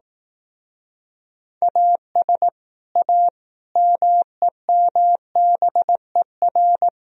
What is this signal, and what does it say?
Morse code, heard as a keyed tone, about 18 wpm, 700 Hz. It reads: AS A MEMBER
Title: Speed-Racing - Sets of 5 Words 15wpm